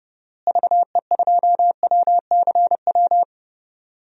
Morse code, heard as a keyed tone, about 31 wpm, 700 Hz. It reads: VE2WCW
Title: DXCC-100 Call Signs - 6 Characters - Repeated 3x - 30wpm